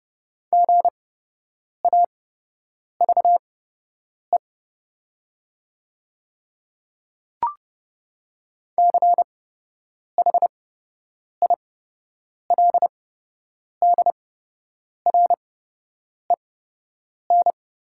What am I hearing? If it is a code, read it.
Morse code, heard as a keyed tone, about 30 wpm, spaced as for 8 wpm, 700 Hz. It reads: GAVE CHILDREN
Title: 8x Character Spacing - Top 500 Words - 30wpm